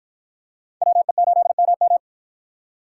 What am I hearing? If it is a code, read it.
Morse code, heard as a keyed tone, about 51 wpm, 700 Hz. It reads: WE8MM